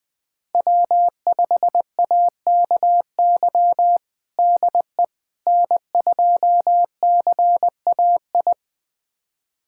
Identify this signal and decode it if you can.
Morse code, heard as a keyed tone, about 20 wpm, 700 Hz. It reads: W5AKY DE N2CAI
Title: USA Callsign DE Callsign - 5 Characters - 20wpm